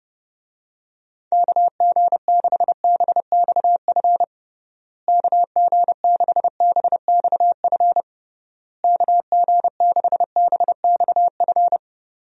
Morse code, heard as a keyed tone, about 30 wpm, 700 Hz. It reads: KG6BXF KG6BXF KG6BXF
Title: USA Call Signs - 6 Characters - Repeated 3x - 30wpm